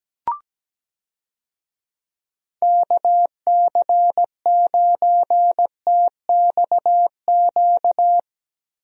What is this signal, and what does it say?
Morse code, heard as a keyed tone, about 17 wpm, 700 Hz. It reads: KC9TXQ
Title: USA Call Signs - 6 Characters - Repeated 2x - 17wpm